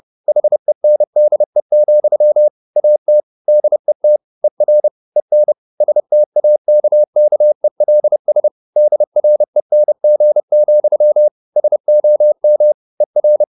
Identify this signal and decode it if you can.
Morse code, heard as a keyed tone, about 30 wpm, 600 Hz. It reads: HENDE, AT DET ER EN STAKKELS DRENG, SOM ER
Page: Ferie 30 wpm